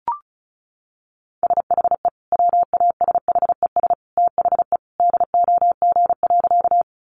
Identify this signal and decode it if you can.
Morse code, heard as a keyed tone, about 34 wpm, 700 Hz. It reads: SHE WASHES THE DOG.
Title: Easy Phrases 35wpm